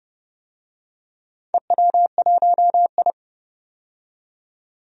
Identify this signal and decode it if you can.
Morse code, heard as a keyed tone, about 31 wpm, 700 Hz. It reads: EW1I
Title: DXCC-100 Call Signs - 4 Characters - 30wpm